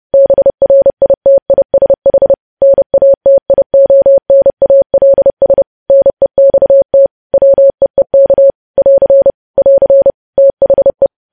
Morse code, heard as a keyed tone, about 30 wpm, 550 Hz. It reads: BRITISH NATIONALS NEXT WEEK <AR> <AR> THE
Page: news headlines in morse code at 30 wpm